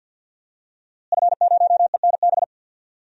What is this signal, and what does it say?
Morse code, heard as a keyed tone, about 48 wpm, 700 Hz. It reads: F0ENB
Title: DXCC-100 Call Signs - 5 Characters - Repeated 3x - 50wpm